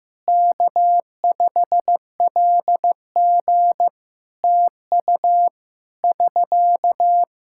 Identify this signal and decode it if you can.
Morse code, heard as a keyed tone, about 15 wpm, 700 Hz. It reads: K5LG TU <SK>
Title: Practice QSO - Exchange 4 - Ending - 15wpm